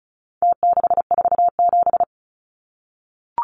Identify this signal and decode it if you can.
Morse code, heard as a keyed tone, about 34 wpm, 700 Hz. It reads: T647
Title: Numbers - Contest Style - 4-Digit 35wpm